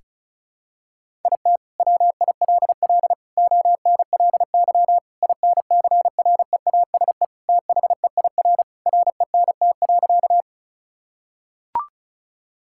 Morse code, heard as a keyed tone, about 36 wpm, 700 Hz. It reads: IT WILL ONLY INCREASE THEIR RENT.
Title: Sentences from Top 1000 Words - Review Part 5 - 35wpm